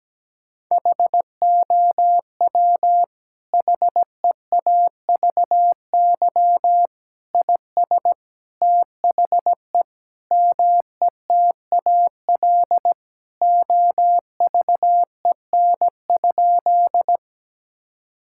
Morse code, heard as a keyed tone, about 17 wpm, 700 Hz. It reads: HOW HEAVY IS THE METAL OVEN?
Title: Sentences from Top 1000 Words - Review Part 6 - 17wpm